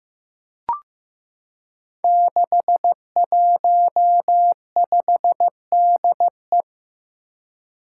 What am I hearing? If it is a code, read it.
Morse code, heard as a keyed tone, about 15 wpm, 700 Hz. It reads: 615DE
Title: Five Challenging Letter-Number Combinations 15wpm